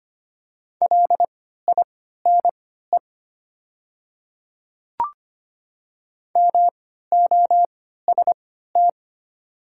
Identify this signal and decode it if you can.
Morse code, heard as a keyed tone, about 25 wpm, spaced as for 14 wpm, 700 Hz. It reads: LINE MOST
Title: 3x Character Spacing - Top 200 Words - 25wpm